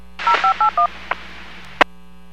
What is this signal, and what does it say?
DTMF, heard as telephone touch tones, tones of about 110 ms. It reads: 0581